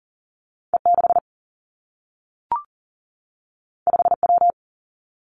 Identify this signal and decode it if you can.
Morse code, heard as a keyed tone, about 42 wpm, 700 Hz. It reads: E 6 5 W
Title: Two Challenging Letter-Number Combinations 40wpm